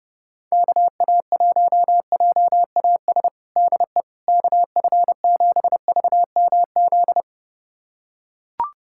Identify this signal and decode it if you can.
Morse code, heard as a keyed tone, about 32 wpm, 700 Hz. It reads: KA1JAS DE KF7VMZ
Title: USA Callsign DE Callsign - 6 Characters - 30wpm